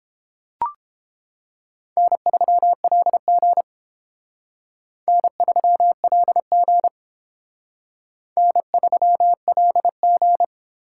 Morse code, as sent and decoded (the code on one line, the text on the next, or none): -. ...-- .-.. --. / -. ...-- .-.. --. / -. ...-- .-.. --.
N3LG N3LG N3LG